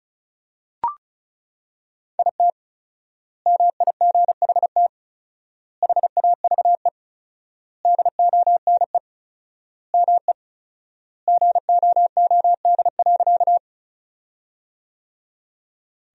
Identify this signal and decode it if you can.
Morse code, heard as a keyed tone, about 36 wpm, 700 Hz. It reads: IT MIGHT HAVE DONE ME GOOD.
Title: 4x Word-Spacing - Sentences from Top 500 Words - 35wpm